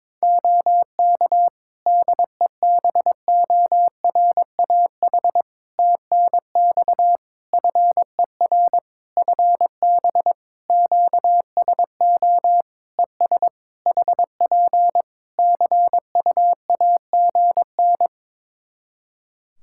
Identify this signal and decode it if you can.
Morse code, heard as a keyed tone, about 22 wpm, 700 Hz. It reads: OK DEBORAH TNX FER FB QSO ES HP CUAGN
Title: Practice QSO - Exchange 4 - Ending - 22wpm